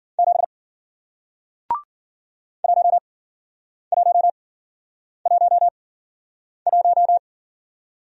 Morse code, heard as a keyed tone, about 50 wpm, spaced as for 9 wpm, 700 Hz. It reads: 7 1111